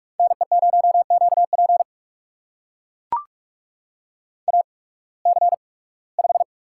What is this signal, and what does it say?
Morse code, heard as a keyed tone, about 44 wpm, spaced as for 23 wpm, 700 Hz. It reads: NE0QP AC5
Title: USA Call Signs - 8x Character Spacing - 45wpm